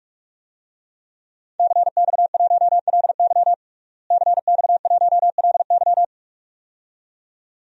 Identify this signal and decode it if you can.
Morse code, heard as a keyed tone, about 46 wpm, 700 Hz. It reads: CX1LY CX1LY
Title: DXCC-100 Call Signs - 5 Characters - Repeated 2x - 45wpm